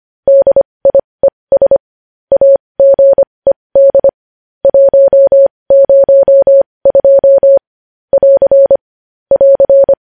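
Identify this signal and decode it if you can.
Morse code, heard as a keyed tone, about 25 wpm, spaced as for 20 wpm, 550 Hz. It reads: DIES AGED 102 <AR> <AR>
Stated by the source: Morse news headlines sent at 20, opening with vvv.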